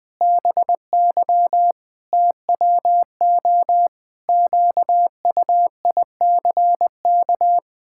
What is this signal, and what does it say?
Morse code, heard as a keyed tone, about 20 wpm, 700 Hz. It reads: BY TWO QUICK
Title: Pangram Challenge v2 - 20wpm